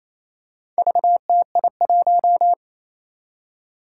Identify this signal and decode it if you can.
Morse code, heard as a keyed tone, about 28 wpm, 700 Hz. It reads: VTI1
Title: Four Challenging Letter-Number Combinations 28wpm